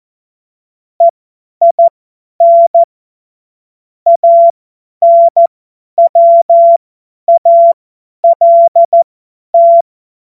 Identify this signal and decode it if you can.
Morse code, heard as a keyed tone, about 14 wpm, spaced as for 10 wpm, 680 Hz. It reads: EIN ANWALT
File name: Morsekurs / Witze / Witz 09. August 2022 50 F70